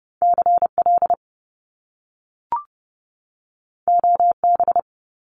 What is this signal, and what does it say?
Morse code, heard as a keyed tone, about 30 wpm, 700 Hz. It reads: CL OB
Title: Top Two Letter Combinations in Words 30wpm